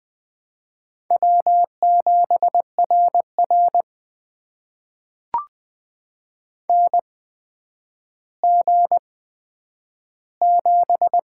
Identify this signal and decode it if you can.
Morse code, heard as a keyed tone, about 21 wpm, spaced as for 10 wpm, 700 Hz. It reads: W7RR NG7
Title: USA Call Signs - 4 Characters - 8x Character Spacing - 20wpm